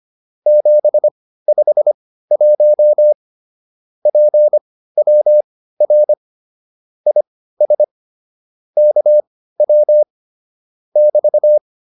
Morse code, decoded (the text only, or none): 751 PWR IS KW <BT>